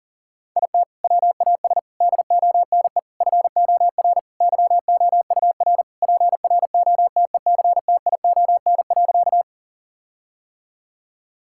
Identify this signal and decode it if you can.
Morse code, heard as a keyed tone, about 37 wpm, 700 Hz. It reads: IT WAS DONE FOR YOUR PROTECTION.
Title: Sentences from Top 1600 Words 40wpm